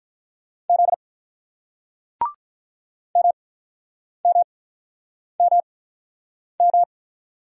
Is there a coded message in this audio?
Yes: Morse code from 0.7 to 6.8 s, heard as a keyed tone, about 48 wpm, spaced as for 9 wpm, 700 Hz. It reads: Z MMMM